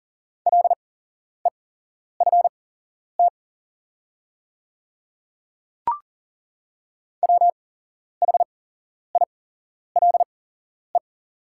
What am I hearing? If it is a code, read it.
Morse code, heard as a keyed tone, about 40 wpm, spaced as for 11 wpm, 700 Hz. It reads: LEFT WHILE
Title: 8x Character Spacing - Top 300 Words - 40wpm